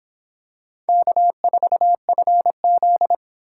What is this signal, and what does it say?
Morse code, heard as a keyed tone, about 26 wpm, 700 Hz. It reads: K4FZ